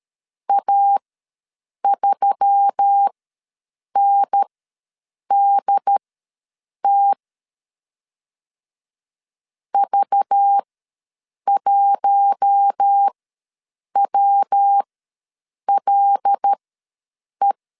Morse code, heard as a keyed tone, about 13 wpm, spaced as for 7 wpm, 800 Hz. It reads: A3NDT V1WLE
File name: LEZIONI / 5 LBVC / 5 LBV ws 12wpm 3015